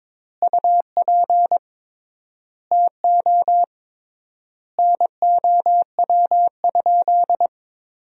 Morse code, heard as a keyed tone, about 22 wpm, 700 Hz. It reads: UP TO NOW?
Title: 3x Word-Spacing - Sentences from Top 100 Words - 22wpm